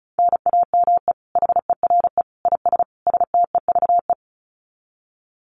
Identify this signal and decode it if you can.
Morse code, heard as a keyed tone, about 34 wpm, 700 Hz. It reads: NAME HERE IS STEVE
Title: Common QSO Phrases 35wpm